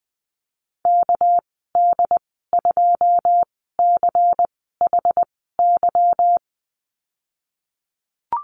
Morse code, heard as a keyed tone, about 20 wpm, 700 Hz. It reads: K D 2 C H Y
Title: 238 Lesson 38g - Letter q - Call Signs 20wpm